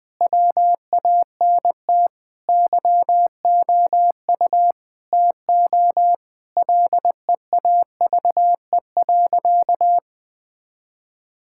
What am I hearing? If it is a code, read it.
Morse code, heard as a keyed tone, about 19 wpm, 700 Hz. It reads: WANT YOU TO LEAVE.
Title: Sentences with 5-Word Phrases - 20wpm